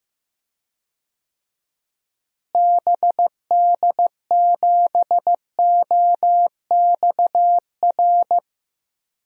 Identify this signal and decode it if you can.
Morse code, heard as a keyed tone, about 16 wpm, 700 Hz. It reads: BD7OXR
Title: DXCC-100 Call Signs - 6 Characters - Repeated 2x - 15wpm